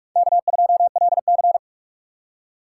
Morse code, heard as a keyed tone, about 45 wpm, 700 Hz. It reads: K2LC
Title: USA Call Signs - 4 Characters - 4x Character Spacing - 45wpm